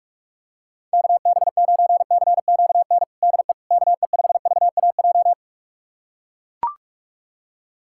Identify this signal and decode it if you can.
Morse code, heard as a keyed tone, about 41 wpm, 700 Hz. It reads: KB9CQN DE KE5VAJ
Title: USA Callsign DE Callsign - 6 Characters - 45wpm